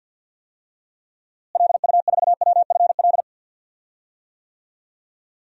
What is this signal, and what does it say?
Morse code, heard as a keyed tone, about 52 wpm, 700 Hz. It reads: LU4WFL
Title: DXCC-100 Call Signs - 6 Characters - Repeated 2x - 50wpm